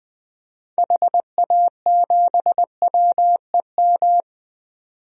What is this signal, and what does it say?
Morse code, heard as a keyed tone, about 20 wpm, 700 Hz. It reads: HA7WEM
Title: DXCC-100 Call Signs - 6 Characters - 20wpm